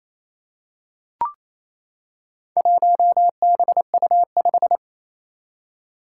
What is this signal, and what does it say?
Morse code, heard as a keyed tone, about 28 wpm, 700 Hz. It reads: 1BU5
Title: Four Challenging Letter-Number Combinations 28wpm